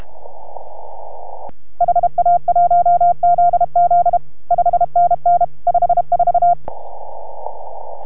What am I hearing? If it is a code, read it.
Morse code, heard as a keyed tone, about 32 wpm, 690 Hz. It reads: HA1ZZ 5NN 54